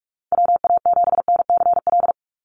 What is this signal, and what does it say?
Morse code, heard as a keyed tone, about 44 wpm, 700 Hz. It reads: WA7NCL
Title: NA SOTA Activator Callsigns 45wpm